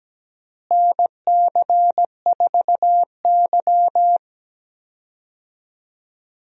Morse code, heard as a keyed tone, about 17 wpm, 700 Hz. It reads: NC4Y
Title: USA Call Signs - 4 Characters - Repeated 2x - 17wpm